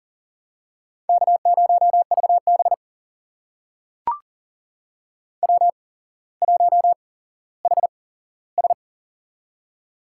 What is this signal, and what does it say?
Morse code, heard as a keyed tone, about 41 wpm, spaced as for 17 wpm, 700 Hz. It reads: K0VB W1HS
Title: USA Call Signs - 4 Characters - 8x Character Spacing - 40wpm